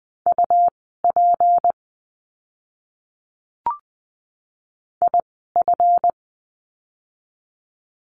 Morse code, heard as a keyed tone, about 20 wpm, spaced as for 15 wpm, 700 Hz. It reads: UP IF
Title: 199 Lesson 32f - Letter k - Words 20wpm